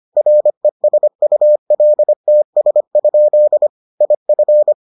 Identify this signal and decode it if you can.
Morse code, heard as a keyed tone, about 25 wpm, 600 Hz. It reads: RESULTS? IF